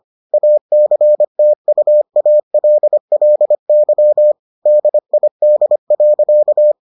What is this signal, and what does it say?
Morse code, heard as a keyed tone, about 25 wpm, 600 Hz. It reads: ACTUALLY DID.